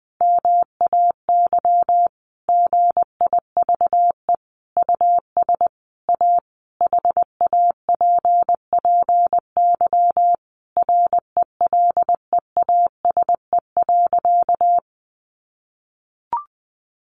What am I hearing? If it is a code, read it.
Morse code, heard as a keyed tone, about 20 wpm, 700 Hz. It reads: MAY GIVE US A HAPPY RELEASE.
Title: Sentences from Top 1100 Words 20wpm